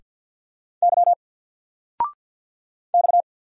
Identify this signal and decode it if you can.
Morse code, heard as a keyed tone, about 47 wpm, 700 Hz. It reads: Y X
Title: Single Letter-Number Alt 50wpm